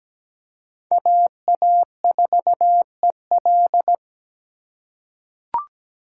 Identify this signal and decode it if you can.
Morse code, heard as a keyed tone, about 17 wpm, 700 Hz. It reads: AA4EL
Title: USA Call Signs - 2x Character Spacing - 17wpm